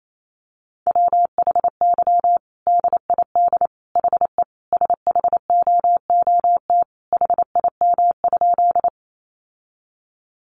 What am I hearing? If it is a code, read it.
Morse code, heard as a keyed tone, about 28 wpm, 700 Hz. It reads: WHY DID HE SHOOT HIM?
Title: Sentences from Top 1200 Words 28wpm